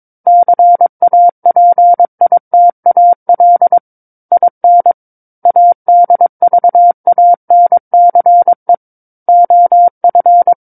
Morse code, heard as a keyed tone, about 22 wpm, 700 Hz. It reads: CAPITAL IN ADVANCE OF
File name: Book One / 13-Chapter 13 - 22 WPM